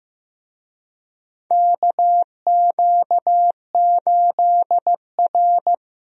Morse code, heard as a keyed tone, about 15 wpm, 700 Hz. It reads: KQ8R
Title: USA Call Signs - 4 Characters - 15wpm